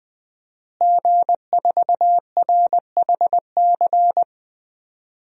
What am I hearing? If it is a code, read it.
Morse code, heard as a keyed tone, about 21 wpm, 700 Hz. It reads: G4RHC
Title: DXCC-100 Call Signs - 5 Characters - Repeated 3x - 20wpm